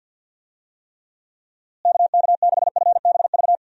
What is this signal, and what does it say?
Morse code, heard as a keyed tone, about 51 wpm, 700 Hz. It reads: KK6FBV
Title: USA Call Signs - 6 Characters - Repeated 3x - 50wpm